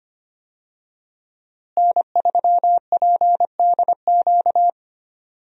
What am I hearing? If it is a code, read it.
Morse code, heard as a keyed tone, about 25 wpm, 700 Hz. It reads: N3PDQ
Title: USA Call Signs - 5 Characters - Repeated 2x - 25wpm